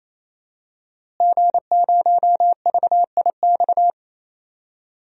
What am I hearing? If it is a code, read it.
Morse code, heard as a keyed tone, about 28 wpm, 700 Hz. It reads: G0VIX
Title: DXCC-100 Call Signs - 5 Characters - Repeated 3x - 28wpm